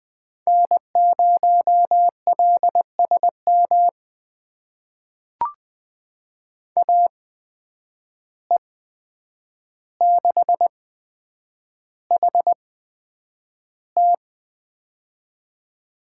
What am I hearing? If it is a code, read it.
Morse code, heard as a keyed tone, about 20 wpm, spaced as for 9 wpm, 700 Hz. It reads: N0LSM AE6HT
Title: USA Call Signs - 5 Characters - 8x Character Spacing - 20wpm